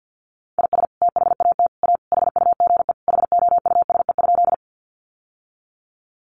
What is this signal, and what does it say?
Morse code, heard as a keyed tone, about 48 wpm, 700 Hz. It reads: IS THAT A HUGE HOUSE?
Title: Sentences from Top 1100 Words 50wpm